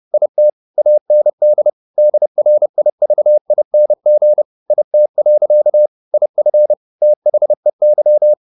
Morse code, heard as a keyed tone, about 30 wpm, 600 Hz. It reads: IT AND DRIVING IT. IF THEY
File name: the old man and the sea / the old man and the sea 1 30wpm